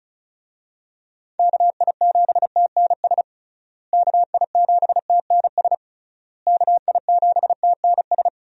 Morse code, heard as a keyed tone, about 36 wpm, 700 Hz. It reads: KI7TNS KI7TNS KI7TNS
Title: USA Call Signs - 6 Characters - Repeated 3x - 35wpm